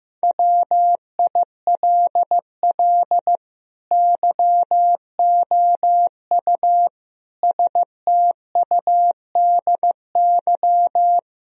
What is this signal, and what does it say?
Morse code, heard as a keyed tone, about 15 wpm, 700 Hz. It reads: WILL YOU STUDY
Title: Sentences from Top 1000 Words - Review Part 3 - 15wpm